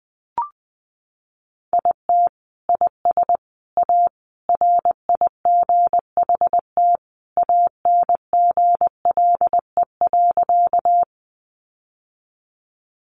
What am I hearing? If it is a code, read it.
Morse code, heard as a keyed tone, about 20 wpm, 700 Hz. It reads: IT IS A RIGHT ANGLE.
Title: Sentences from Top 1500 Words 20wpm